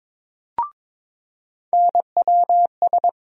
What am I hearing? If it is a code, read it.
Morse code, heard as a keyed tone, about 22 wpm, 700 Hz. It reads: NWS
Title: USA Call Sign Postfixes - 3 Letters 22wpm